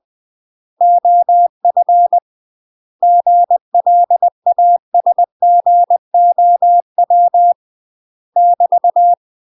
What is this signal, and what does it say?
Morse code, heard as a keyed tone, about 20 wpm, 700 Hz. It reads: OF GLASGOW <BT>
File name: UK MASTER QSOs TRAINING / 20WPM / UK MASTER QSOs 0014